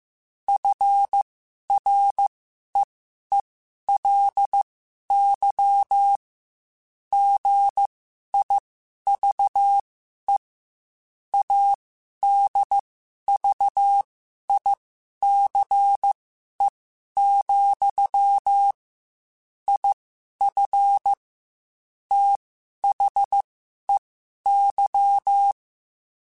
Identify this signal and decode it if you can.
Morse code, heard as a keyed tone, about 15 wpm, spaced as for 11 wpm, 800 Hz. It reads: FREELY GIVE ADVICE, IF THEY